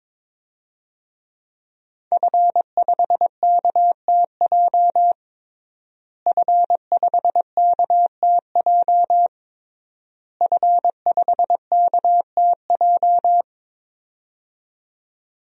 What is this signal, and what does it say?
Morse code, heard as a keyed tone, about 22 wpm, 700 Hz. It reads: F5KTJ F5KTJ F5KTJ
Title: DXCC-100 Call Signs - 5 Characters - Repeated 3x - 22wpm